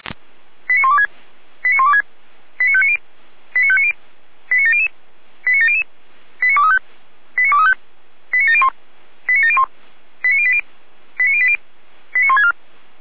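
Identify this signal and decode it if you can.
Five-tone selective call (ZVEI), heard as quick runs of five tones, tones of about 70 ms. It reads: 89126 89126 89590 89590 89700 89700 89235 89235 89701 89701 89080 89080 89164
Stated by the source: Five-tone ZVEI selective call received by SDR.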